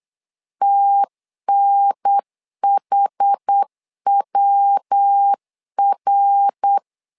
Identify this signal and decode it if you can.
Morse code, heard as a keyed tone, about 8 wpm, 800 Hz. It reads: TNHWR
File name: 4 Q 8wpm